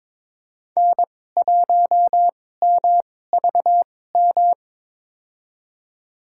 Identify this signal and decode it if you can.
Morse code, heard as a keyed tone, about 21 wpm, spaced as for 16 wpm, 700 Hz. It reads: N1MVM
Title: USA Call Signs - 2x Character Spacing - 22wpm